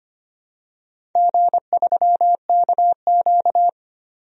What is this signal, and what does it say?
Morse code, heard as a keyed tone, about 25 wpm, 700 Hz. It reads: G3KQ
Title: DXCC-100 Call Signs - 4 Characters - 25wpm